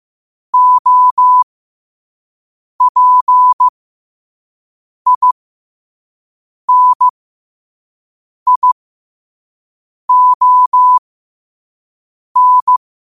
Morse code, heard as a keyed tone, about 15 wpm, spaced as for 5 wpm, 1000 Hz. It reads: OPINION
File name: qotdfiles / 5wpm